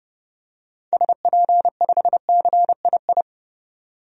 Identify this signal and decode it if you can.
Morse code, heard as a keyed tone, about 30 wpm, 700 Hz. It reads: SP5CII
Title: DXCC-100 Call Signs - 6 Characters - 30wpm